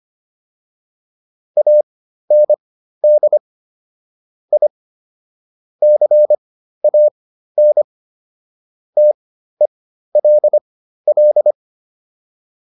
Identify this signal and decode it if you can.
Morse code, heard as a keyed tone, about 25 wpm, spaced as for 13 wpm, 600 Hz. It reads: AND I CAN TELL